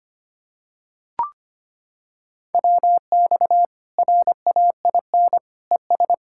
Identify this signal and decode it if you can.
Morse code, heard as a keyed tone, about 25 wpm, 700 Hz. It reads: WX RAIN ES
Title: Practice QSO - Exchange 2 - Info - 25wpm